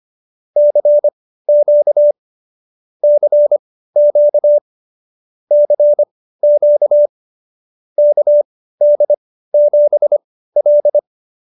Morse code, heard as a keyed tone, about 25 wpm, spaced as for 15 wpm, 590 Hz. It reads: CQ CQ CQ KD7L
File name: qso222 15